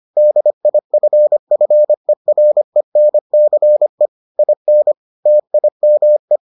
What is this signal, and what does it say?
Morse code, heard as a keyed tone, about 25 wpm, 600 Hz. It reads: DIFFERENCE IN TIME